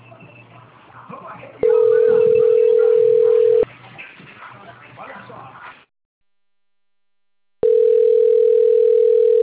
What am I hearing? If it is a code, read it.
Ringback tone (North America), 440 + 480 Hz.
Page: Limeira IV